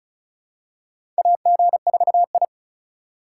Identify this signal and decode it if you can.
Morse code, heard as a keyed tone, about 36 wpm, 700 Hz. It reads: AG4I
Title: USA Call Signs - 4 Characters - 3x Character Spacing - 35wpm